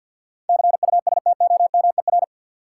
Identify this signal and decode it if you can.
Morse code, heard as a keyed tone, about 52 wpm, 700 Hz. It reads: CUSTOMER
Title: Sentences from Top 1100 Words - New Word Drill Alt - 50wpm